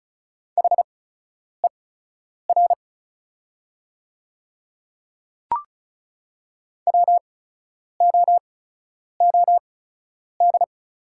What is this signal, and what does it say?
Morse code, heard as a keyed tone, about 36 wpm, spaced as for 10 wpm, 700 Hz. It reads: HER WOOD
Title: 8x Character Spacing - Top 300 Words - 35wpm